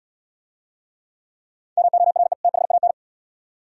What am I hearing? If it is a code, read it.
Morse code, heard as a keyed tone, about 67 wpm, 700 Hz. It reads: MORE THAN